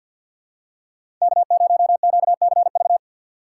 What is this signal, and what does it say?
Morse code, heard as a keyed tone, about 52 wpm, 700 Hz. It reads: K0QCV
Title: USA Call Signs - 5 Characters - 3x Character Spacing - 50wpm